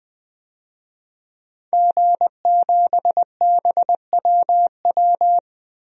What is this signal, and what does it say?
Morse code, heard as a keyed tone, about 20 wpm, 700 Hz. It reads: G7BWW
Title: DXCC-100 Call Signs - 5 Characters - Repeated 2x - 20wpm